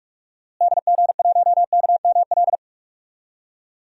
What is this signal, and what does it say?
Morse code, heard as a keyed tone, about 46 wpm, 700 Hz. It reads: DG1KML
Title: DXCC-100 Call Signs - 6 Characters - Repeated 3x - 45wpm